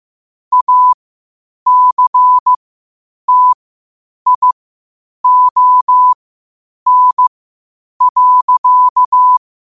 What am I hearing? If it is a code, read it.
Morse code, heard as a keyed tone, about 15 wpm, spaced as for 8 wpm, 1000 Hz. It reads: ACTION.
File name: qotdfiles / 8wpm